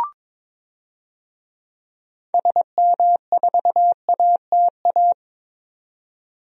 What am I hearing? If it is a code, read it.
Morse code, heard as a keyed tone, about 21 wpm, 700 Hz. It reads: SM4ATA
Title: DXCC-100 Call Signs - 6 Characters - 22wpm